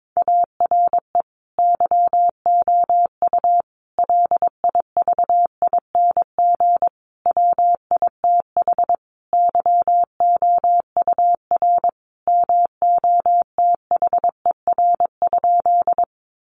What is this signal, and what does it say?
Morse code, heard as a keyed tone, about 22 wpm, 700 Hz. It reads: ARE YOU LIVING WITH YOUR MOTHER?
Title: Sentences from Top 1100 Words 22wpm